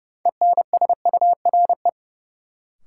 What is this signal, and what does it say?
Morse code, heard as a keyed tone, about 30 wpm, 700 Hz. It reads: ENSURE
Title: Sentences from Top 1100 Words - New Word Drill Alt - 30wpm